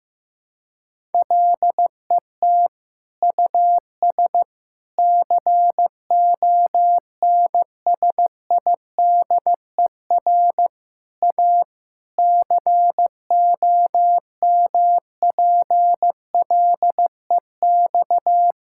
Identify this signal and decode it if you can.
Morse code, heard as a keyed tone, about 16 wpm, 700 Hz. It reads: LET US CONSIDER A COMPLEX
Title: Sentences from Top 1000 Words - Review Part 6 - 15wpm